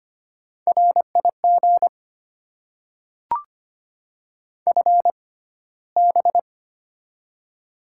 Morse code, heard as a keyed tone, about 25 wpm, spaced as for 15 wpm, 700 Hz. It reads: RIG FB
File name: Top 90 QSO Elements - 6x Character Spacing - 25wpm